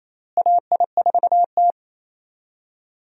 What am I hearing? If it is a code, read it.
Morse code, heard as a keyed tone, about 27 wpm, 700 Hz. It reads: AI4T
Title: USA Call Signs - 4 Characters - Repeated 3x - 28wpm